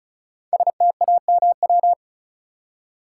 Morse code, heard as a keyed tone, about 36 wpm, 700 Hz. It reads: STAMW
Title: Code Groups - 5 Letters - 35wpm